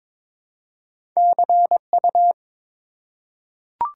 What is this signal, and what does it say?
Morse code, heard as a keyed tone, about 22 wpm, 700 Hz. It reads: CU
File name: Top 90 QSO Elements - 4x Character Spacing - 22wpm